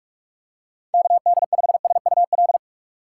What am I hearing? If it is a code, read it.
Morse code, heard as a keyed tone, about 46 wpm, 700 Hz. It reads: KD5SUL